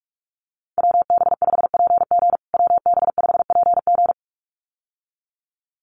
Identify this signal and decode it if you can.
Morse code, heard as a keyed tone, about 44 wpm, 700 Hz. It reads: WB5PG WB5PG
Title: USA Call Signs Repeated 2x 45wpm